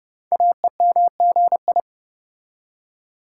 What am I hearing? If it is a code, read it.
Morse code, heard as a keyed tone, about 30 wpm, 700 Hz. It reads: AEMGI
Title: Code Groups - 5 Letters - 30wpm